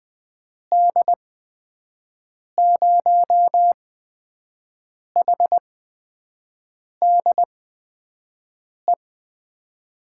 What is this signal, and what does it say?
Morse code, heard as a keyed tone, about 21 wpm, spaced as for 6 wpm, 700 Hz. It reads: D0HDE
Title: USA Call Signs - 6 Characters - 8x Character Spacing - 20wpm